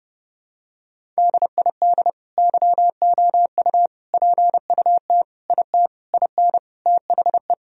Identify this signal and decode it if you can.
Morse code, heard as a keyed tone, about 32 wpm, 700 Hz. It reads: DID YOU PUT IT IN THE
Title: Sentences from Top 1000 Words - Review Part 4 - 30wpm